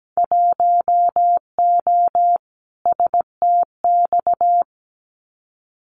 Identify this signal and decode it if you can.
Morse code, heard as a keyed tone, about 17 wpm, 700 Hz. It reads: 1O STX
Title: Winter Field Day Exchanges 17wpm